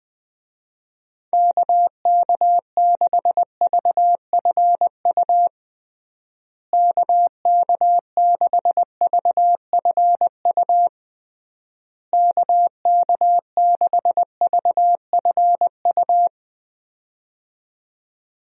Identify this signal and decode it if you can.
Morse code, heard as a keyed tone, about 20 wpm, 700 Hz. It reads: KK6VFU KK6VFU KK6VFU
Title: USA Call Signs - 6 Characters - Repeated 3x - 20wpm